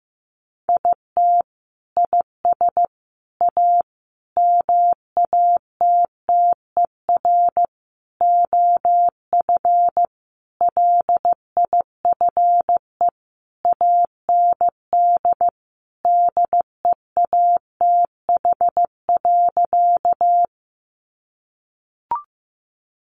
Morse code, heard as a keyed tone, about 15 wpm, 700 Hz. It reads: IT IS A MATTER OF LIFE AND DEATH.
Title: Sentences from Top 1100 Words 15wpm